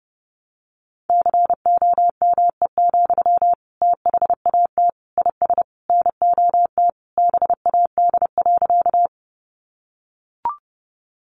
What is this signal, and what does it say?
Morse code, heard as a keyed tone, about 30 wpm, 700 Hz. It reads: COME, THAT IS NOT BAD.
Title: Sentences from Top 700 Words 30wpm